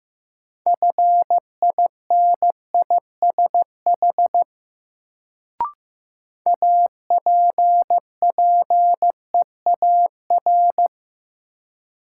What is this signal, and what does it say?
Morse code, heard as a keyed tone, about 15 wpm, 700 Hz. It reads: FINISH APPEAR
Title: Sentences from Top 1200 Words - New Word Drill Alt - 15wpm